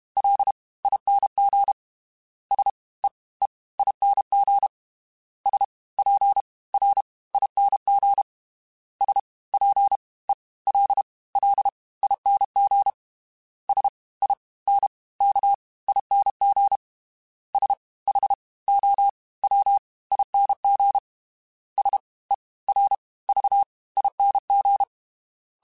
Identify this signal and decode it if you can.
Morse code, heard as a keyed tone, about 32 wpm, 800 Hz. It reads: L ING S E E ING S P R ING S P E L L ING S I N K ING S H O W ING S E R V ING
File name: Special INGs / Special INGs-mp3s / Fast ing @30